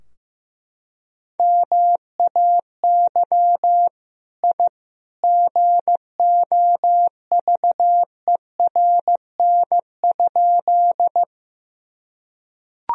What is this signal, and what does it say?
Morse code, heard as a keyed tone, about 15 wpm, 700 Hz. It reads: MAY I GOVERN?
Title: Sentences from Top 1000 Words - Review Part 2 - 15wpm